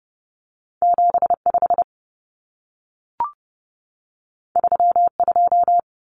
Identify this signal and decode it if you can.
Morse code, heard as a keyed tone, about 30 wpm, 700 Hz. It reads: 75 32
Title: Common Cut Numbers - 2-Digit 30wpm